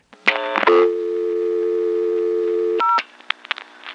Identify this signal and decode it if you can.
Dial tone (North America), 350 + 440 Hz.